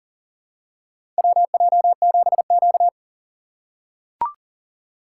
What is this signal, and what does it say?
Morse code, heard as a keyed tone, about 40 wpm, 700 Hz. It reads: WJ7Q
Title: USA Call Signs - 4 Characters - 4x Character Spacing - 40wpm